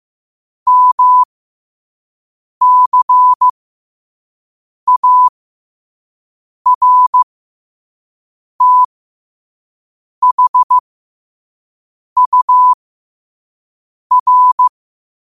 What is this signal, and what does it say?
Morse code, heard as a keyed tone, about 15 wpm, spaced as for 5 wpm, 1000 Hz. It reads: MCARTHUR